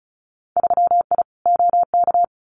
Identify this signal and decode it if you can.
Morse code, heard as a keyed tone, about 36 wpm, 700 Hz. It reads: 3I OK
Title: Winter Field Day Exchanges Encore 35wpm